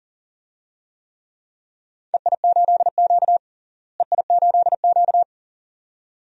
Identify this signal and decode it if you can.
Morse code, heard as a keyed tone, about 40 wpm, 700 Hz. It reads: EI8Q EI8Q
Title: DXCC-100 Call Signs - 4 Characters - Repeated 2x - 40wpm